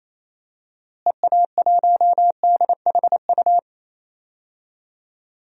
Morse code, heard as a keyed tone, about 27 wpm, 700 Hz. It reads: EA1DHU
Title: DXCC-100 Call Signs - 6 Characters - Repeated 3x - 28wpm